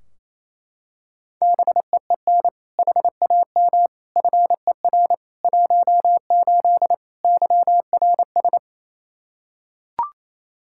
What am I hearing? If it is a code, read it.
Morse code, heard as a keyed tone, about 28 wpm, 700 Hz. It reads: BEEN HAM FER 18 YRS
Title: Practice QSO - Exchange 3 - More Info - 28wpm